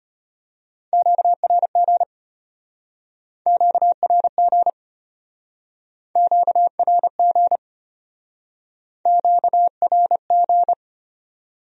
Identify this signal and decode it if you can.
Morse code, heard as a keyed tone, about 32 wpm, 700 Hz. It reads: QRG QRG QRG QRG